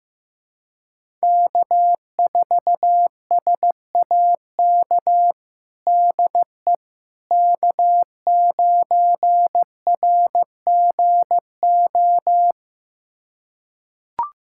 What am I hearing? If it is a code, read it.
Morse code, heard as a keyed tone, about 15 wpm, 700 Hz. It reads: K4SAK DE K9RGO
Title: USA Callsign DE Callsign - 5 Characters - 15wpm